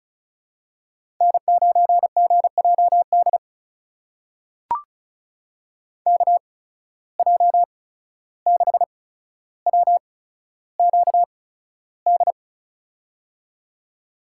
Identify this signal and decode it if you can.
Morse code, heard as a keyed tone, about 36 wpm, spaced as for 14 wpm, 700 Hz. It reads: N9GJD KJ6WQD
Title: USA Call Signs - 8x Character Spacing - 35wpm